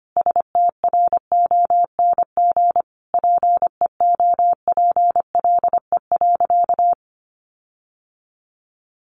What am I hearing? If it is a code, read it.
Morse code, heard as a keyed tone, about 25 wpm, 700 Hz. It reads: STRONG PEOPLE.
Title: Sentences from Top 1400 Words 25wpm